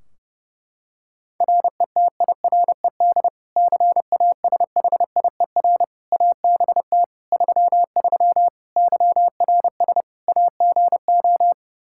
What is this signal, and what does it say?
Morse code, heard as a keyed tone, about 29 wpm, 700 Hz. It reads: RETIRED CASHIER ABT 33 YRS AGO
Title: Practice QSO - Exchange 3 - More Info - 30wpm